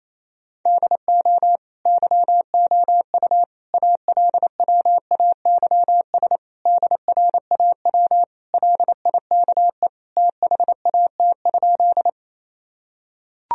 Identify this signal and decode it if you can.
Morse code, heard as a keyed tone, about 27 wpm, 700 Hz. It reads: DO YOU ALWAYS DRAW LIKE THAT?
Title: Sentences from Top 1000 Words - Review Part 1 - 28wpm